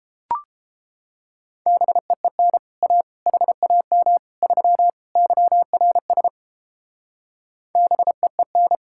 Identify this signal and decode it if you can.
Morse code, heard as a keyed tone, about 31 wpm, 700 Hz. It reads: BEEN A HAM 3 YRS BEEN